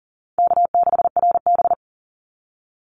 Morse code, heard as a keyed tone, about 38 wpm, 700 Hz. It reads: K6RB
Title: Easy Call Signs - 40wpm